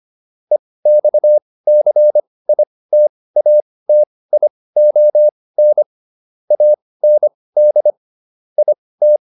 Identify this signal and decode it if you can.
Morse code, heard as a keyed tone, about 25 wpm, spaced as for 18 wpm, 600 Hz. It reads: EXCITATION AND IT
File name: SS211 18